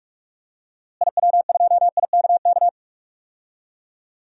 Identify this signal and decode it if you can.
Morse code, heard as a keyed tone, about 46 wpm, 700 Hz. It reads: IW2IKK
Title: DXCC-100 Call Signs - 6 Characters - Repeated 2x - 45wpm